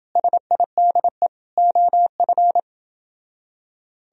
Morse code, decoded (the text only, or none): SIDE OF